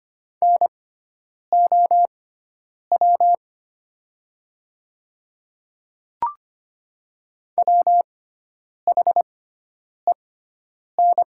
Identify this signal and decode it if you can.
Morse code, heard as a keyed tone, about 25 wpm, spaced as for 9 wpm, 700 Hz. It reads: NOW WHEN
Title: 6x Character Spacing - Top 100 Words - 25wpm